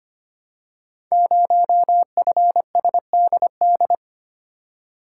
Morse code, heard as a keyed tone, about 25 wpm, 700 Hz. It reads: 0FSDD
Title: Code Groups - 5 Letter:Numbers - 25wpm